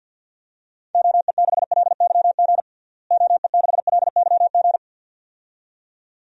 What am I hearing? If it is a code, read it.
Morse code, heard as a keyed tone, about 48 wpm, 700 Hz. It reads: OE6LYG OE6LYG
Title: DXCC-100 Call Signs - 6 Characters - Repeated 2x - 50wpm